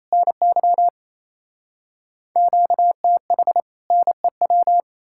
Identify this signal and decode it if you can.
Morse code, heard as a keyed tone, about 37 wpm, 700 Hz. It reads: NY QTH NEW